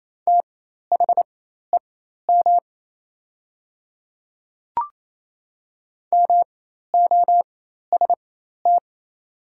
Morse code, heard as a keyed tone, about 28 wpm, spaced as for 13 wpm, 700 Hz. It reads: THEM MOST
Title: 4x Character Spacing - Top 100 Words - 28wpm